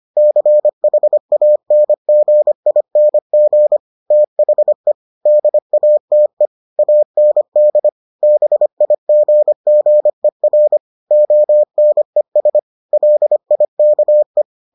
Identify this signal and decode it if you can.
Morse code, heard as a keyed tone, about 25 wpm, 600 Hz. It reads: CHANGING THE DATE AND BIGGER ONES LIKE